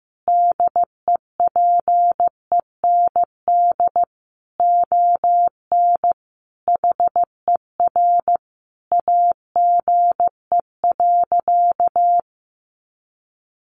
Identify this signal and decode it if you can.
Morse code, heard as a keyed tone, about 15 wpm, 700 Hz. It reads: DEPEND ON HER AGE.
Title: Sentences from Top 1300 Words 15wpm